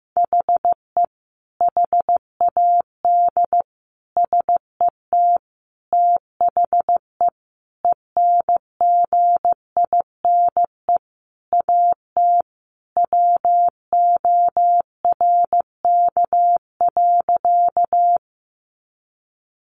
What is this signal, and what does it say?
Morse code, heard as a keyed tone, about 15 wpm, 700 Hz. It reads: HE HAD SET THE ENGINE AT WORK.
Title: Sentences from Top 1500 Words 15wpm